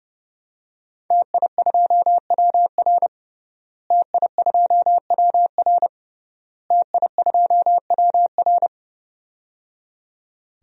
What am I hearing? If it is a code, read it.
Morse code, heard as a keyed tone, about 30 wpm, 700 Hz. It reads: TI2WR TI2WR TI2WR
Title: DXCC-100 Call Signs - 5 Characters - Repeated 3x - 30wpm